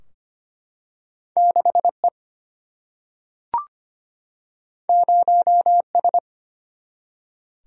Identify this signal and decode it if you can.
Morse code, heard as a keyed tone, about 25 wpm, 700 Hz. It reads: 6 E 0 S
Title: Single-Letter Single-Number 25wpm